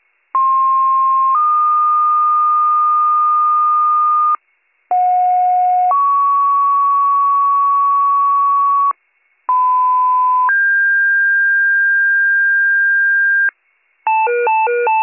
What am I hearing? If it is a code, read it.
Two-tone page, heard as a one-second tone, then a three-second tone, about 1060 then 1220 Hz, about 730 then 1120 Hz, about 1010 then 1600 Hz. It is heard from 0.3 to 13.5 s.